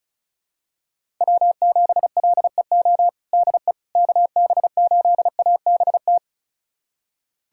Morse code, heard as a keyed tone, about 36 wpm, 700 Hz. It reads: W7LEO DE KB8ABT
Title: USA Callsign DE Callsign 35wpm